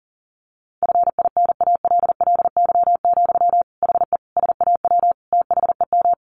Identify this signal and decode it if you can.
Morse code, heard as a keyed tone, about 40 wpm, 700 Hz. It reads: FINALLY, HE SAW THEM
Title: Sentences from Top 1300 Words 40wpm